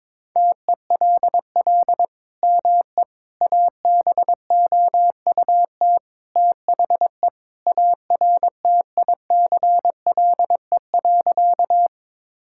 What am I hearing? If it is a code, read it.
Morse code, heard as a keyed tone, about 22 wpm, 700 Hz. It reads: TELL ME ABOUT THE ARTICLE.
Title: Sentences from Top 1000 Words - Review Part 5 - 22wpm